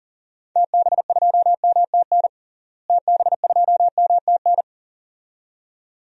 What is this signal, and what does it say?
Morse code, heard as a keyed tone, about 40 wpm, 700 Hz. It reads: TB2MTN TB2MTN
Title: DXCC-100 Call Signs - 6 Characters - Repeated 2x - 40wpm